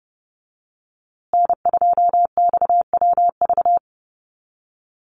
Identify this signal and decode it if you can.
Morse code, heard as a keyed tone, about 30 wpm, 700 Hz. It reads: N2XWV
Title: USA Call Signs Repeated 3x 30wpm